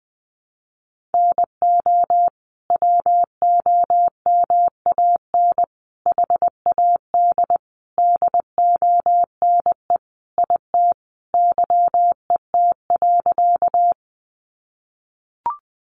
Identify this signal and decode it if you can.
Morse code, heard as a keyed tone, about 20 wpm, 700 Hz. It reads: NO WOMAN HAD DONE IT YET.
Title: Sentences from Top 1100 Words 20wpm